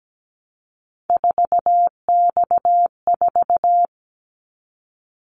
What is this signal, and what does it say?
Morse code, heard as a keyed tone, about 17 wpm, 700 Hz. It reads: 4X4
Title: Callsign Prefixes for Top 50 DXCC - 17wpm